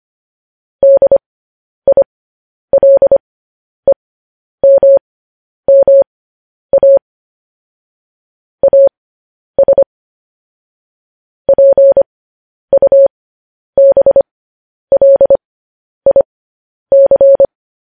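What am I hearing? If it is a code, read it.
Morse code, heard as a keyed tone, about 25 wpm, spaced as for 10 wpm, 550 Hz. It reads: DILEMMA AS PUBLIC
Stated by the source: Morse news headlines at 10.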